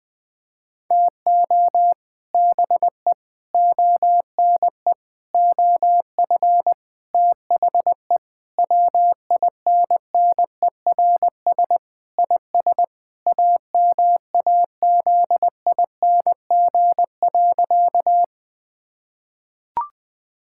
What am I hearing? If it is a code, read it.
Morse code, heard as a keyed tone, about 20 wpm, 700 Hz. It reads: TO BE ONE OF THE WINNERS IS AMAZING.
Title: Sentences with 5-Word Phrases - Spoken Beginning - 20wpm